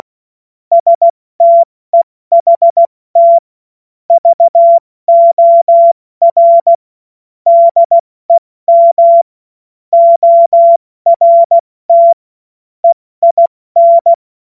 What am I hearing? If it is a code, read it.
Morse code, heard as a keyed tone, about 16 wpm, 680 Hz. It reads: STEHT VOR DEM ORT EIN
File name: Morsekurs / Witze / Witz 20. September 2022 70 F80